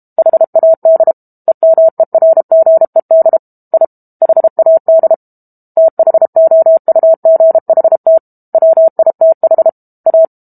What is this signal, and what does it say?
Morse code, heard as a keyed tone, about 32 wpm, 650 Hz. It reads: HAD EMERGED I HAD THOUGHT WITH A
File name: Book Two / 06-Chapter 6 - 32 WPM